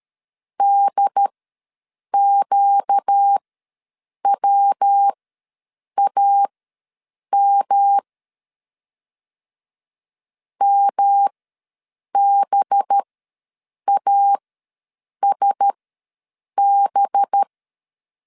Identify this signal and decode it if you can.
Morse code, heard as a keyed tone, about 13 wpm, spaced as for 7 wpm, 800 Hz. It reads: DQWAM MBASB
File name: 5 LB ws 12wpm